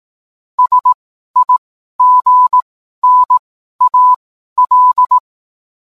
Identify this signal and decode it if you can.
Morse code, heard as a keyed tone, about 18 wpm, spaced as for 12 wpm, 1000 Hz. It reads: SIGNAL